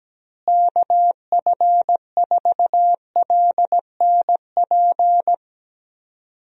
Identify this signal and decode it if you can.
Morse code, heard as a keyed tone, about 17 wpm, 700 Hz. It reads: KF4LNP
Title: USA Call Signs - 6 Characters - Repeated 3x - 17wpm